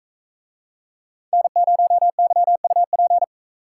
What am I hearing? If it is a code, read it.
Morse code, heard as a keyed tone, about 44 wpm, 700 Hz. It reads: N0YUP